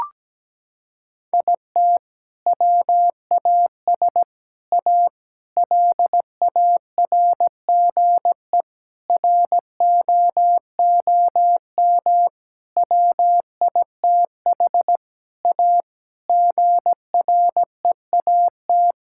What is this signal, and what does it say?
Morse code, heard as a keyed tone, about 17 wpm, 700 Hz. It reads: IT WAS A LARGE ROOM WITH A GREAT
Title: Sentences from Top 1600 Words 17wpm